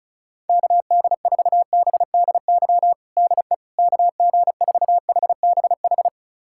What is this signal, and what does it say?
Morse code, heard as a keyed tone, about 36 wpm, 700 Hz. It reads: KD4BDY DE KG4HBH
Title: USA Callsign DE Callsign 35wpm